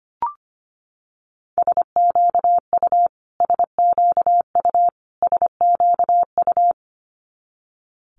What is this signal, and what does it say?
Morse code, heard as a keyed tone, about 25 wpm, 700 Hz. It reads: SQU SQU SQU
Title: Multi-letter Phonemes Repeated 3x 25wpm